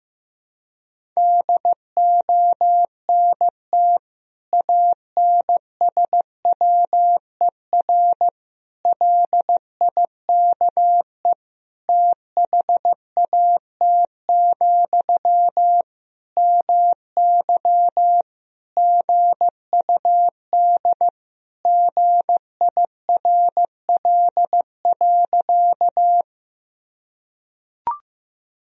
Morse code, heard as a keyed tone, about 15 wpm, 700 Hz. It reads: DONT ANSWER LIKE THAT, MY GUD GIRL.
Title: Sentences from Top 400 Words and Abbreviations 15wpm